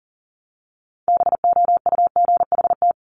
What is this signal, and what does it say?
Morse code, heard as a keyed tone, about 39 wpm, 700 Hz. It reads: BOUGHT
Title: Multi-letter Phonemes Repeated 3x 40wpm